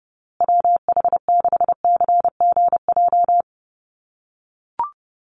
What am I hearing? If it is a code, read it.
Morse code, heard as a keyed tone, about 30 wpm, 700 Hz. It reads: WH6CGJ
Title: USA Call Signs Repeated 3x 30wpm